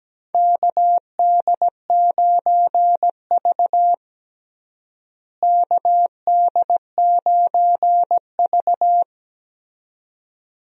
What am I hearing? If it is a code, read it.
Morse code, heard as a keyed tone, about 17 wpm, 700 Hz. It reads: KD9V KD9V
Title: USA Call Signs - 4 Characters - Repeated 2x - 17wpm